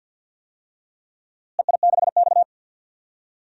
Morse code, heard as a keyed tone, about 50 wpm, 700 Hz. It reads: EI6X
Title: DXCC-100 Call Signs - 4 Characters - Repeated 3x - 50wpm